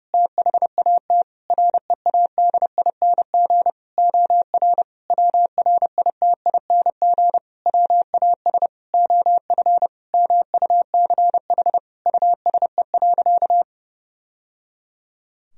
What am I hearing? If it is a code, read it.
Morse code, heard as a keyed tone, about 30 wpm, 700 Hz. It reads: THAT READING OR WRITING WAS OF MUCH USE.
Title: Sentences from Top 1000 Words - Review Part 3 - 30wpm